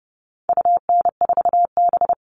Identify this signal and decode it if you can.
Morse code, heard as a keyed tone, about 30 wpm, 700 Hz. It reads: UN4B
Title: All Cut Numbers - 4-Digit 30wpm